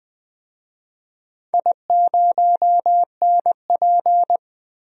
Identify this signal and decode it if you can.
Morse code, heard as a keyed tone, about 20 wpm, 700 Hz. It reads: I0NP
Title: DXCC-100 Call Signs - 4 Characters - 20wpm